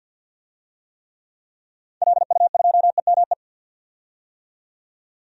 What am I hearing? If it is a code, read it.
Morse code, heard as a keyed tone, about 51 wpm, 700 Hz. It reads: LU2EME
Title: DXCC-100 Call Signs - 6 Characters - 50wpm